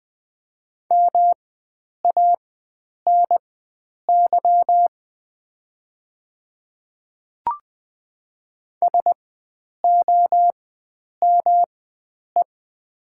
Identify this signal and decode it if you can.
Morse code, heard as a keyed tone, about 20 wpm, spaced as for 9 wpm, 700 Hz. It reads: MANY SOME
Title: 4x Character Spacing - Top 100 Words - 20wpm